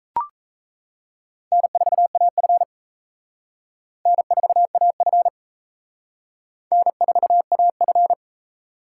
Morse code, heard as a keyed tone, about 34 wpm, 700 Hz. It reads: N4AF N4AF N4AF
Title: Speed Racing - USA 1x2 Call Signs 28wpm